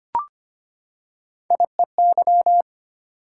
Morse code, heard as a keyed tone, about 25 wpm, 700 Hz. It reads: IEY
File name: Three Letter-Number Combinations - 25wpm